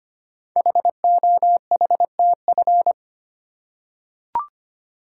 Morse code, heard as a keyed tone, about 25 wpm, 700 Hz. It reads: HOHTF
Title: Code Groups - 5 Letters - 25wpm